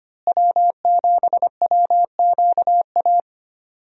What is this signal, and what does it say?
Morse code, heard as a keyed tone, about 25 wpm, 700 Hz. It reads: W7WQA
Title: USA Call Signs - 5 Characters - Repeated 3x - 25wpm